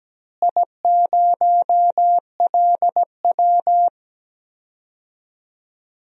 Morse code, heard as a keyed tone, about 17 wpm, 700 Hz. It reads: I0LW
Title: DXCC-100 Call Signs - 4 Characters - Repeated 3x - 17wpm